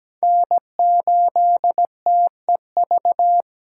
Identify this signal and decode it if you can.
Morse code, heard as a keyed tone, about 17 wpm, 700 Hz. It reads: N8TEV
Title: USA Call Signs - 5 Characters - 3x Character Spacing - 17wpm